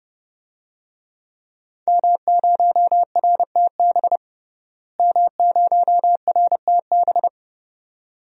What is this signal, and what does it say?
Morse code, heard as a keyed tone, about 30 wpm, 700 Hz. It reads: M0RTB M0RTB
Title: DXCC-100 Call Signs - 5 Characters - Repeated 2x - 30wpm